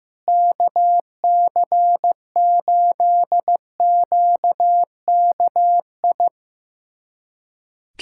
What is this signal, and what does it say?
Morse code, heard as a keyed tone, about 15 wpm, 700 Hz. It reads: KC8QKI
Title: USA Callsign DE Callsign 15wpm